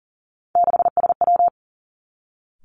Morse code, heard as a keyed tone, about 40 wpm, 700 Hz. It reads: 6SW
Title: Three Challenging Letter-Number Combinations 40wpm